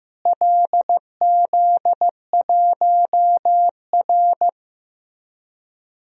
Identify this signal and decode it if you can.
Morse code, heard as a keyed tone, about 15 wpm, 700 Hz. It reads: LZ1R
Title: DXCC-100 Call Signs - 4 Characters - Repeated 2x - 15wpm